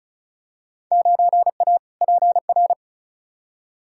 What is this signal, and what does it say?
Morse code, heard as a keyed tone, about 36 wpm, 700 Hz. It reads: 9A PR
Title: Field Day Exchanges Encore - 35wpm